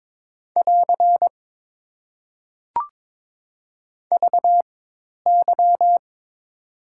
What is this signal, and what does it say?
Morse code, heard as a keyed tone, about 22 wpm, spaced as for 10 wpm, 700 Hz. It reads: <AR> VY
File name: Top 90 QSO Elements - 4x Character Spacing - 22wpm